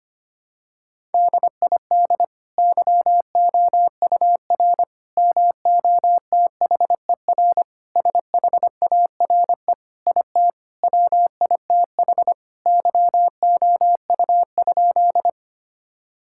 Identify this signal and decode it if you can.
Morse code, heard as a keyed tone, about 25 wpm, 700 Hz. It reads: DID YOUR MOTHER SHARE IT WITH YOU?
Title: Sentences from Top 1000 Words - Review Part 6 - 25wpm